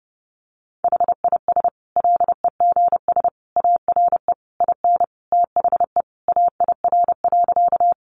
Morse code, heard as a keyed tone, about 30 wpm, 700 Hz. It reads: HIS LEGS ARE IN THE AIR.
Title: Additional Easy Phrases 30wpm